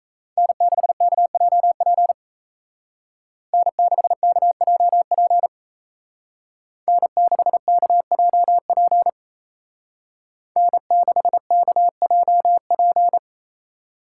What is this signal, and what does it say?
Morse code, heard as a keyed tone, about 37 wpm, 700 Hz. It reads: N6KJP N6KJP N6KJP N6KJP